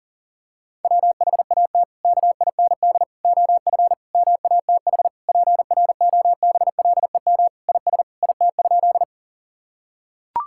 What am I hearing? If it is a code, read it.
Morse code, heard as a keyed tone, about 37 wpm, 700 Hz. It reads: WHAT KIND OF MATH PROBLEM IS IT?
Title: Sentences from Top 1000 Words - Review Part 6 - 40wpm